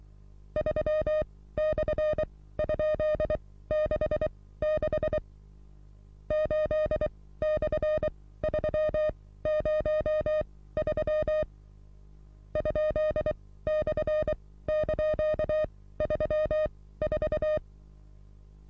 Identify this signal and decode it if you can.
Morse code, heard as a keyed tone, about 23 wpm, spaced as for 16 wpm, 620 Hz. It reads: 3/?66 8/303 ?/)34
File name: repos / Morsekurs / DJ4UF Morsen27